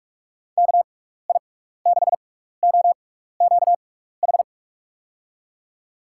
Morse code, heard as a keyed tone, about 44 wpm, spaced as for 15 wpm, 700 Hz. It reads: KI6OQH
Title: USA Call Signs - 6 Characters - 6x Character Spacing - 45wpm